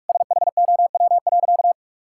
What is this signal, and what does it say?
Morse code, heard as a keyed tone, about 46 wpm, 700 Hz. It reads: SHOW.
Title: Sentences from Top 1000 Words - Review Part 6 - 45wpm